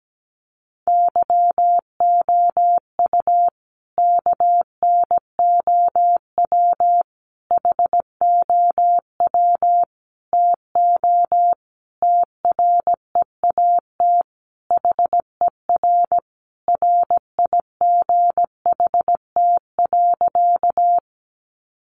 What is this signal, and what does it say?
Morse code, heard as a keyed tone, about 17 wpm, 700 Hz. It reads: YOU KNOW HOW TO TREAT HER RIGHT.
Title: Sentences from Top 1300 Words 17wpm